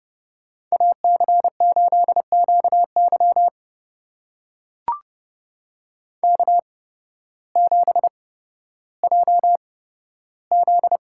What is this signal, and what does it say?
Morse code, heard as a keyed tone, about 30 wpm, spaced as for 14 wpm, 700 Hz. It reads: AC8QY K7JZ
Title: USA Call Signs - 5 Characters - 8x Character Spacing - 30wpm